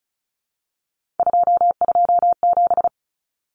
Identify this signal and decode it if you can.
Morse code, heard as a keyed tone, about 34 wpm, 700 Hz. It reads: 227
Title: Common Cut Numbers - 3-Digit 35wpm